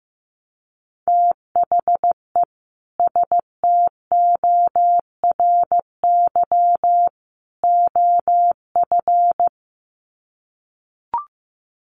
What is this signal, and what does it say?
Morse code, heard as a keyed tone, about 15 wpm, 700 Hz. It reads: THE STORY OF
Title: Sets of 3 Words 15wpm